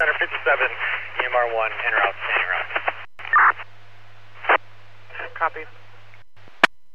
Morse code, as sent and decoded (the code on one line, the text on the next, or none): none